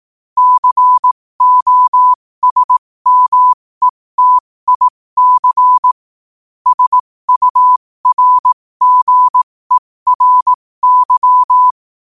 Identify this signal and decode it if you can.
Morse code, heard as a keyed tone, about 18 wpm, spaced as for 15 wpm, 1000 Hz. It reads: COSMETIC SURGERY
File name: qotdfiles / 15wpm